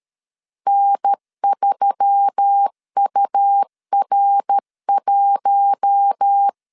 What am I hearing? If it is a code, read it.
Morse code, heard as a keyed tone, about 13 wpm, 800 Hz. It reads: N3UR1
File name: CW / DET / 1 6 1500C S B 12wpm 3431